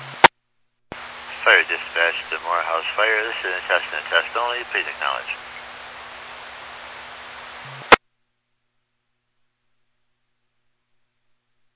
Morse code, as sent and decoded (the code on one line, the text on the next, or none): none